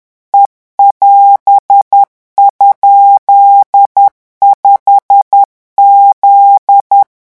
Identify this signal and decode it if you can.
Morse code, heard as a keyed tone, about 10 wpm, 800 Hz. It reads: E<AS>?5Z
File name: morse mp3 / revision19-vitesse 10 mots